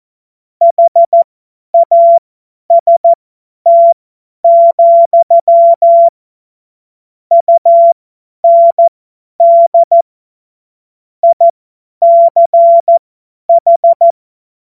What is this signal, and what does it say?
Morse code, heard as a keyed tone, about 14 wpm, spaced as for 10 wpm, 680 Hz. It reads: HAST, UND ICH
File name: Morsekurs / Witze / Witz 10. August 2021, 50 F70